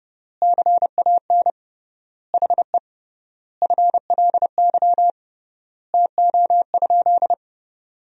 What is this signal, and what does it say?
Morse code, heard as a keyed tone, about 30 wpm, 700 Hz. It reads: CAN HE FLY TO?
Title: 3x Word-Spacing - Sentences from Top 500 Words - 30wpm